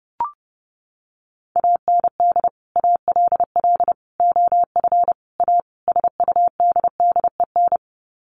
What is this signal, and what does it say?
Morse code, heard as a keyed tone, about 30 wpm, 700 Hz. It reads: AND ALL OF A SUDDEN
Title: Sets of 5 Words 30wpm